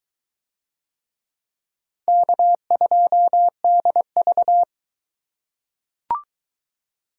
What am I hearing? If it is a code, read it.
Morse code, heard as a keyed tone, about 23 wpm, 700 Hz. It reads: K2DV